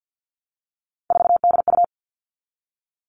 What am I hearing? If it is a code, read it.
Morse code, heard as a keyed tone, about 50 wpm, 700 Hz. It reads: 4DU
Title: All Cut Numbers - 3-Digit 50wpm